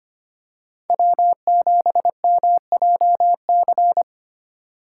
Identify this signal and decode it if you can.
Morse code, heard as a keyed tone, about 25 wpm, 700 Hz. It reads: W7MJC
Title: USA Call Signs - 5 Characters - 4x Character Spacing - 25wpm